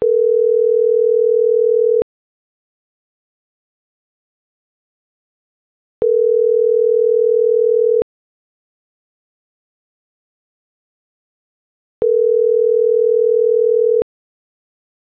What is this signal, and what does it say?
Ringback tone (North America), 440 + 480 Hz.